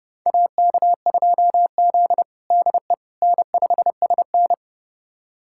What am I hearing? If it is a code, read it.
Morse code, heard as a keyed tone, about 30 wpm, 700 Hz. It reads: AK2Z DE N5SN
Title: USA Callsign DE Callsign - 4 Characters - 30wpm